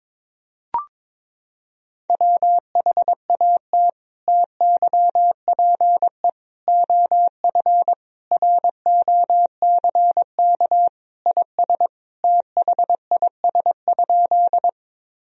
Morse code, heard as a keyed tone, about 21 wpm, 700 Hz. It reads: WHAT TYPE OF ROCK IS THIS?
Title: Sentences from Top 1000 Words - Review Part 3 - 22wpm